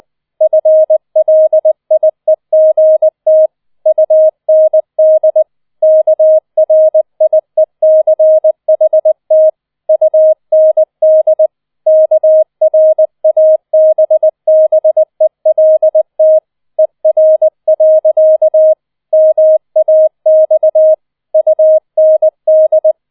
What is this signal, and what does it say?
Morse code, heard as a keyed tone, about 20 wpm, 630 Hz. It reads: FLIEGT UND KRIECHT UND KRABBELT ER. MAX UND
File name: 20wpm